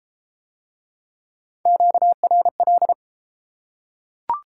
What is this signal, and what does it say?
Morse code, heard as a keyed tone, about 28 wpm, 700 Hz. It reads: QRL